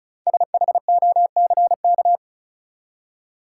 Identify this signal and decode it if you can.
Morse code, heard as a keyed tone, about 34 wpm, 700 Hz. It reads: SHOCK
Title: Five Letter Words - 35wpm